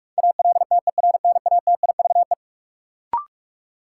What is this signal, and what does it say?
Morse code, heard as a keyed tone, about 43 wpm, 700 Hz. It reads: ALTERNATIVE
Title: Sentences from Top 1000 Words - Review Part 4 - New Word Drill - 45wpm